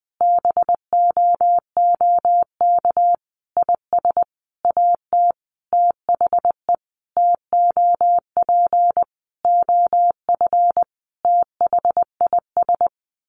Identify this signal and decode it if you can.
Morse code, heard as a keyed tone, about 20 wpm, 700 Hz. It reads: BOOK IS AT THE TOP OF THIS
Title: Sentences from Top 1200 Words 20wpm